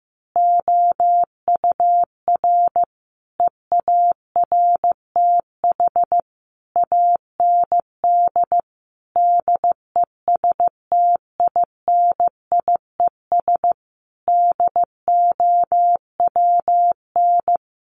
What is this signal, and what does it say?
Morse code, heard as a keyed tone, about 15 wpm, 700 Hz. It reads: OUR EARTH AND DESTINIES DOWN
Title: A Sound Of Thunder 15wpm